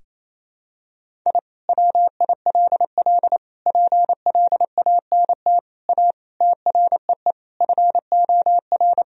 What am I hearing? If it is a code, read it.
Morse code, heard as a keyed tone, about 28 wpm, 700 Hz. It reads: I WILL PLANT A TREE FOR
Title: Sentences from Top 1000 Words - Review Part 1 - 28wpm